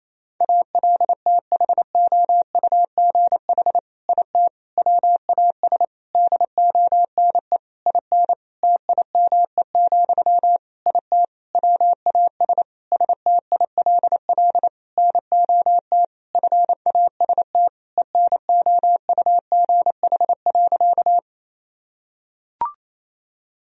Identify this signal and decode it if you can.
Morse code, heard as a keyed tone, about 27 wpm, 690 Hz. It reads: ALTHOUGH IT WAS DONE IN TIME, IT WAS STILL NOT FAST ENOUGH.
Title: Sentences from Top 1000 Words - Review Part 3 - 28wpm